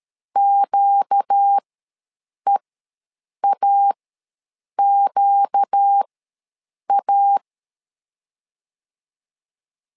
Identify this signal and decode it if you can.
Morse code, heard as a keyed tone, about 13 wpm, spaced as for 7 wpm, 800 Hz. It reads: QEAQA
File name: LEZIONI / 4 Q ws 12wpm 1912